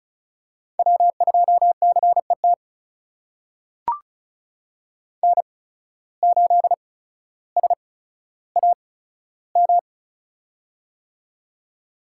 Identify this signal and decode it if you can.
Morse code, heard as a keyed tone, about 36 wpm, spaced as for 15 wpm, 700 Hz. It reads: W2CET N8SAM
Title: USA Call Signs - 5 Characters - 8x Character Spacing - 35wpm